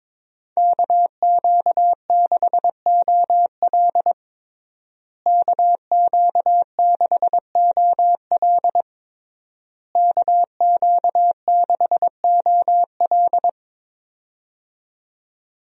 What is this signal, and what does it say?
Morse code, heard as a keyed tone, about 22 wpm, 700 Hz. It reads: KQ6OL KQ6OL KQ6OL
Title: USA Call Signs - 5 Characters - Repeated 3x - 22wpm